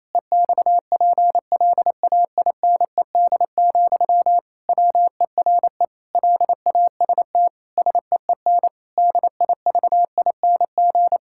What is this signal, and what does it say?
Morse code, heard as a keyed tone, about 28 wpm, 700 Hz. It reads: EXPLAINED, WERE LAST SEEN DIVING
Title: Pangram Challenge v2 - 28wpm